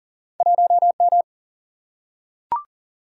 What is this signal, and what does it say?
Morse code, heard as a keyed tone, about 39 wpm, 700 Hz. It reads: 1M
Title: Single-Letter Single-Number 40wpm